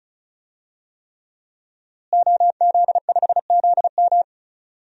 Morse code, heard as a keyed tone, about 36 wpm, 700 Hz. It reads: OZ5ZM
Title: DXCC-100 Call Signs - 5 Characters - Repeated 3x - 35wpm